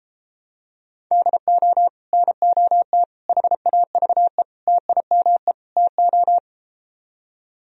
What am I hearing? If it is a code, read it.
Morse code, heard as a keyed tone, about 33 wpm, 700 Hz. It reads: DO NOT HAVE TIME TO